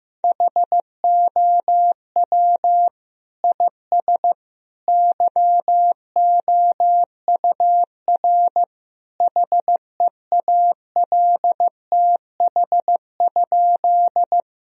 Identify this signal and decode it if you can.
Morse code, heard as a keyed tone, about 15 wpm, 700 Hz. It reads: HOW IS YOUR HEALTH?
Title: Sentences from Top 1000 Words - Review Part 3 - 15wpm